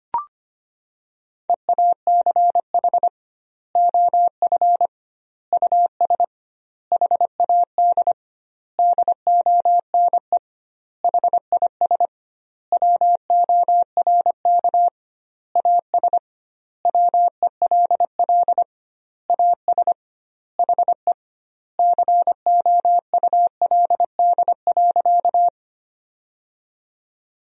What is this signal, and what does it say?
Morse code, heard as a keyed tone, about 25 wpm, 700 Hz. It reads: EACH OF US HAD DONE HIS WORK AS WELL AS HE COULD.
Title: 2x Word-Spacing - Sentences from Top 500 Words - 25wpm